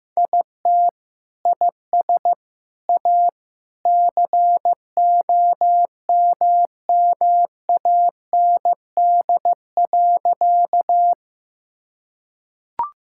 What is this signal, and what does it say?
Morse code, heard as a keyed tone, about 16 wpm, 700 Hz. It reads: IT IS A COMMAND.
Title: Sentences from Top 1600 Words 15wpm